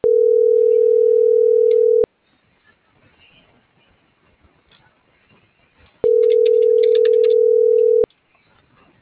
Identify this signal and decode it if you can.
Ringback tone (North America), 440 + 480 Hz.